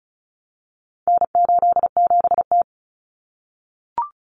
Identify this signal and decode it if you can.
Morse code, heard as a keyed tone, about 34 wpm, 700 Hz. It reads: N87T
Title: Common Cut Numbers - 4-Digit 35wpm